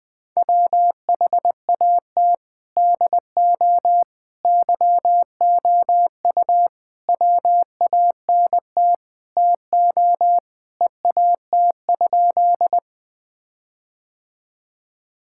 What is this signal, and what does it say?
Morse code, heard as a keyed tone, about 20 wpm, 700 Hz. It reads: WHAT DO YOU WANT TO EAT?
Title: Sentences with 5-Word Phrases - 20wpm